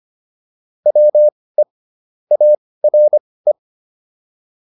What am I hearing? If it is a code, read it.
Morse code, heard as a keyed tone, about 25 wpm, spaced as for 18 wpm, 600 Hz. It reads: WE ARE
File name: WD104 18